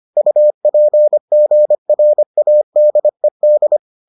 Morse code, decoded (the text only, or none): UPGRADED